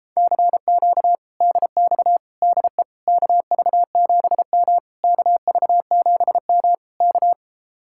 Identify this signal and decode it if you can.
Morse code, heard as a keyed tone, about 37 wpm, 700 Hz. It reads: CQ DX DE KV7M KV7M K